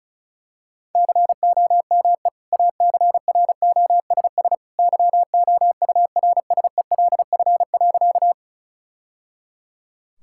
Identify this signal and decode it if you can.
Morse code, heard as a keyed tone, about 36 wpm, 700 Hz. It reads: COME ACROSS YOURSELF.
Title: Sentences from Top 1000 Words - Review Part 5 - 35wpm